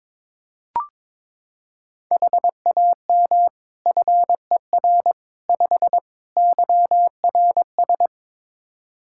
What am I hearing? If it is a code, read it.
Morse code, heard as a keyed tone, about 22 wpm, 700 Hz. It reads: HAM FER 5 YRS
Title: Practice QSO - Exchange 3 - More Info - 22wpm